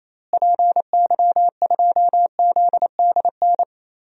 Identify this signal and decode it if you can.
Morse code, heard as a keyed tone, about 28 wpm, 700 Hz. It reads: PY2ZDN
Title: DXCC-100 Call Signs - 6 Characters - Repeated 3x - 28wpm